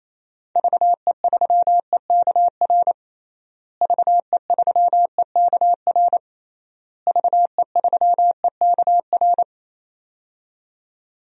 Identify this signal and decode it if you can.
Morse code, heard as a keyed tone, about 28 wpm, 700 Hz. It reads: VE3EKR VE3EKR VE3EKR
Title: DXCC-100 Call Signs - 6 Characters - Repeated 3x - 28wpm